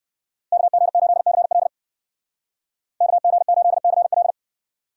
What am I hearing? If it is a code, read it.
Morse code, heard as a keyed tone, about 60 wpm, 700 Hz. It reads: KG8CL KG8CL